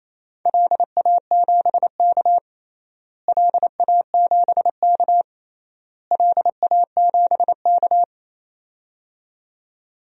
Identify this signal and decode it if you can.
Morse code, heard as a keyed tone, about 28 wpm, 700 Hz. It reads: LA7K LA7K LA7K
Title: DXCC-100 Call Signs - 4 Characters - Repeated 3x - 28wpm